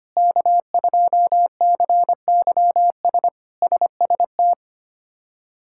Morse code, heard as a keyed tone, about 25 wpm, 700 Hz. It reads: K2CYS SST
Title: K1USN SST QSO Practice - 25wpm